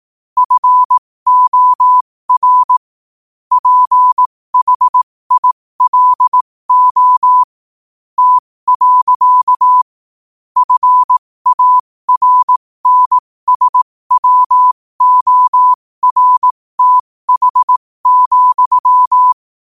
Morse code, heard as a keyed tone, about 18 wpm, spaced as for 15 wpm, 1000 Hz. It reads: FOR PHILO T. FARNSWORTH,
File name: qotdfiles / 15wpm